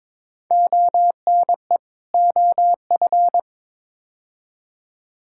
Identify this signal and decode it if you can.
Morse code, heard as a keyed tone, about 22 wpm, 700 Hz. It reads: ONE OF
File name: practice / 22wpm / Build A Phrase - 5-Word Phrases - 22wpm